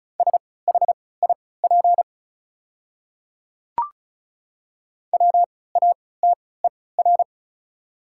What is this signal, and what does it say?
Morse code, heard as a keyed tone, about 36 wpm, spaced as for 20 wpm, 700 Hz. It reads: SHIP WATER
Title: 3x Character Spacing - Top 500 Words - 35wpm